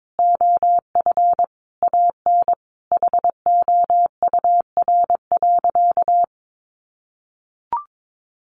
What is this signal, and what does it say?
Morse code, heard as a keyed tone, about 22 wpm, 700 Hz. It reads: OF AN HOUR.
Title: Sentences from Top 1500 Words 22wpm